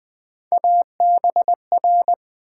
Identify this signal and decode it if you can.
Morse code, heard as a keyed tone, about 20 wpm, 700 Hz. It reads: ABR
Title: Build A Word - 3 Letters At A Time - 6-Letter Words - 20wpm